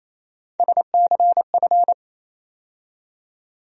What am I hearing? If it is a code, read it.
Morse code, heard as a keyed tone, about 28 wpm, 700 Hz. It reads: SCF
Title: USA Call Sign Postfixes - 3 Letters 28wpm